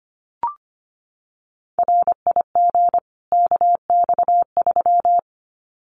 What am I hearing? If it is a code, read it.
Morse code, heard as a keyed tone, about 25 wpm, 700 Hz. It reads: RIG KX3
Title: Common QSO Phrases 25wpm